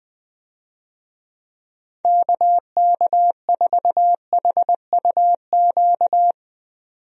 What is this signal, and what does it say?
Morse code, heard as a keyed tone, about 20 wpm, 700 Hz. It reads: KK4HUQ
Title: USA Call Signs - 6 Characters - 20wpm